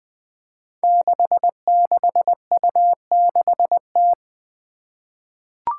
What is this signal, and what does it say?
Morse code, heard as a keyed tone, about 20 wpm, 700 Hz. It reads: 66U6T
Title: Five Challenging Letter-Number Combinations 20wpm